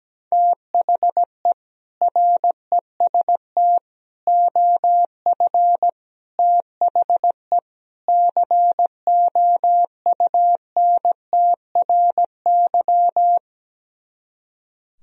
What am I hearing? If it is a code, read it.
Morse code, heard as a keyed tone, about 17 wpm, 700 Hz. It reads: THE REST OF THE COUNTRY
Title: Speed-Racing - Sets of 5 Words 17wpm